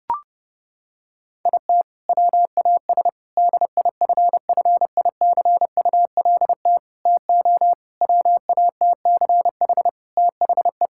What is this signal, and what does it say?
Morse code, heard as a keyed tone, about 30 wpm, 700 Hz. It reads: IT WAS DIFFICULT TO WATCH THE
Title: Sentences from Top 1000 Words - Review Part 5 - 30wpm